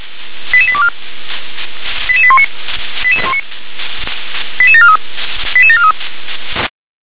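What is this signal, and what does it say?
Five-tone selective call (ZVEI), heard as quick runs of five tones, tones of about 70 ms. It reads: 88824 88819 88743 88743
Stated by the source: Five-tone ZVEI selective call received by SDR.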